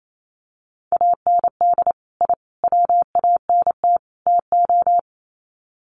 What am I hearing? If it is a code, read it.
Morse code, heard as a keyed tone, about 28 wpm, 700 Hz. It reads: AND I WANT TO
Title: Sets of 4 Words 28wpm